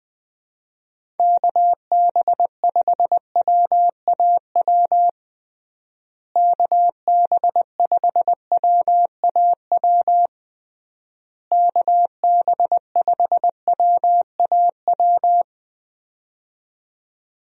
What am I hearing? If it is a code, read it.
Morse code, heard as a keyed tone, about 20 wpm, 700 Hz. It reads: KB5WAW KB5WAW KB5WAW
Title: USA Call Signs - 6 Characters - Repeated 3x - 20wpm